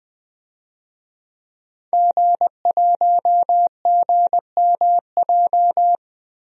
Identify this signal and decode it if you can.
Morse code, heard as a keyed tone, about 20 wpm, 700 Hz. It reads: G1GMJ
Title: DXCC-100 Call Signs - 5 Characters - Repeated 2x - 20wpm